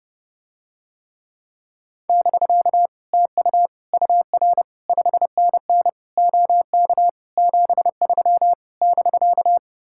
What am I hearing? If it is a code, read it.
Morse code, heard as a keyed tone, about 30 wpm, 700 Hz. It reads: <BK> TU UR 5NN OK 73 <BK>